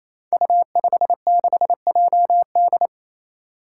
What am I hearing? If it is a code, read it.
Morse code, heard as a keyed tone, about 28 wpm, 700 Hz. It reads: U56JD
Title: Five Challenging Letter-Number Combinations 28wpm